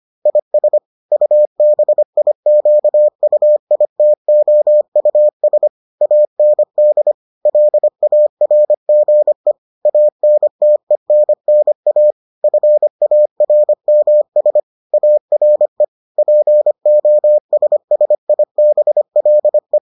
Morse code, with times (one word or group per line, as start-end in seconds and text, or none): 0.3-0.8 IS
1.1-5.7 UBIQUITOUS
6.0-7.1 AND
7.5-9.5 LARGE
9.9-12.1 ANTENNA
12.4-14.6 FARMS
14.9-15.8 ARE
16.2-19.8 POSSIBLE